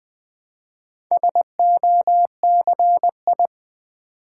Morse code, heard as a keyed tone, about 20 wpm, 690 Hz. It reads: SOCI
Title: Build A Word - 2 Letters At A Time - 6-Letter Words - 20wpm